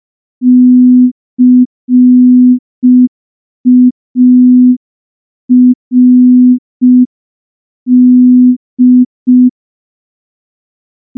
Morse code, heard as a keyed tone, about 5 wpm, 250 Hz. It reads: CARD
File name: morse / morse-training-05WPM